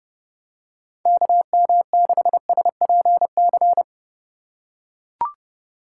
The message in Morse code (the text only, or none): KM6SPC